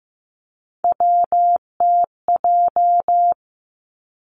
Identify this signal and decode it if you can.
Morse code, heard as a keyed tone, about 15 wpm, 700 Hz. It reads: WTJ
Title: Three Challenging Letter-Number Combinations 15wpm